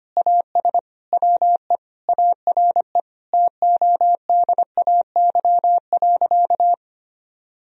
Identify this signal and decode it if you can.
Morse code, heard as a keyed tone, about 25 wpm, 690 Hz. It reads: AS WE ARE TODAY.
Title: Sentences from Top 1000 Words - Review Part 4 - 25wpm